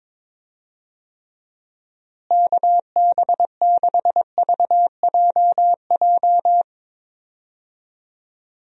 Morse code, heard as a keyed tone, about 21 wpm, 700 Hz. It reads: KB6VJJ
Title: USA Call Signs - 6 Characters - Repeated 3x - 22wpm